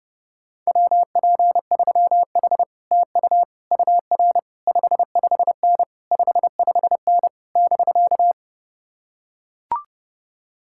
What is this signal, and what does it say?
Morse code, heard as a keyed tone, about 30 wpm, 700 Hz. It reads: WP3H TU UR 55N 55N <BK>